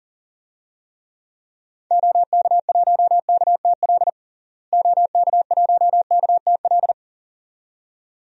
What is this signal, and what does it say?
Morse code, heard as a keyed tone, about 37 wpm, 700 Hz. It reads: OK1KTL OK1KTL
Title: DXCC-100 Call Signs - 6 Characters - Repeated 2x - 40wpm